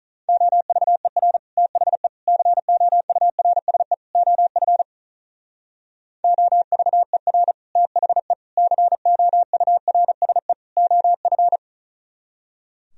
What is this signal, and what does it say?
Morse code, heard as a keyed tone, about 44 wpm, 700 Hz. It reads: OVER THE COURSE OF OVER THE COURSE OF